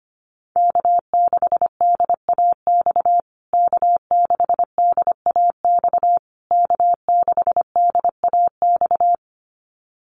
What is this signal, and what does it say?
Morse code, heard as a keyed tone, about 25 wpm, 700 Hz. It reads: K6DAX K6DAX K6DAX
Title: USA Call Signs Repeated 3x 25wpm